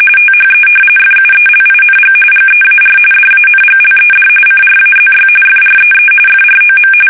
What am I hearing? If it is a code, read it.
RTTY, about 45.45 baud. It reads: THE FUELING GEAR DANGLED CRAZY FREE FROM
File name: fsk850 / battleshipnj